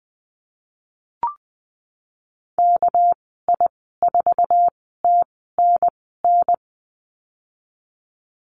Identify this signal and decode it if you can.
Morse code, heard as a keyed tone, about 20 wpm, spaced as for 15 wpm, 700 Hz. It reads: KI4TNN
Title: 200 Lesson 32g - Letter k - Call Signs 20wpm